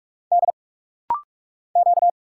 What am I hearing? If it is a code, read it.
Morse code, heard as a keyed tone, about 46 wpm, 700 Hz. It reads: D Q
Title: Single Letters - Rapid-Fire 45wpm